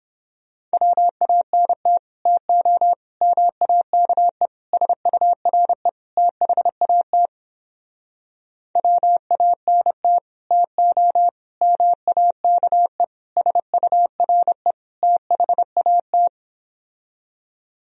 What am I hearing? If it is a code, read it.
Morse code, heard as a keyed tone, about 29 wpm, 700 Hz. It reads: WANT TO MAKE SURE THAT WANT TO MAKE SURE THAT